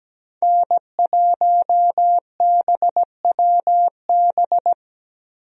Morse code, heard as a keyed tone, about 17 wpm, 700 Hz. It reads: N1BWB
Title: USA Call Signs - 5 Characters - 17wpm